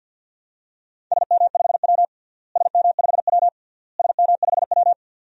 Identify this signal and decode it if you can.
Morse code, heard as a keyed tone, about 51 wpm, 700 Hz. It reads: SM5W SM5W SM5W
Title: DXCC-100 Call Signs - 4 Characters - Repeated 3x - 50wpm